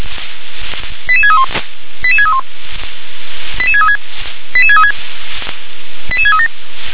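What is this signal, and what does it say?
Five-tone selective call (ZVEI), heard as quick runs of five tones, tones of about 70 ms. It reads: 88631 88631 88636 88636 88637
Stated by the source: ZVEI five-tone selective call received on SDR.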